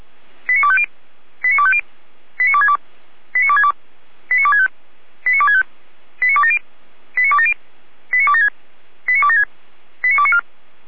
Five-tone selective call (ZVEI), heard as quick runs of five tones, tones of about 70 ms. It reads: 89250 89250 89262 89262 89265 89265 89270 89270 89276 89276 89294